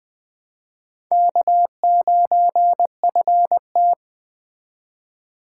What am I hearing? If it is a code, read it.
Morse code, heard as a keyed tone, about 22 wpm, 700 Hz. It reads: K9FT